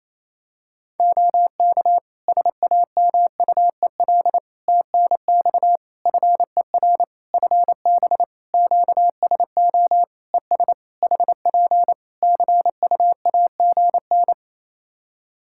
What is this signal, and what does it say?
Morse code, heard as a keyed tone, about 28 wpm, 700 Hz. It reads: OK SAMUEL TNX FER FB QSO ES HP CUAGN
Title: K1USN SST QSO Practice 28wpm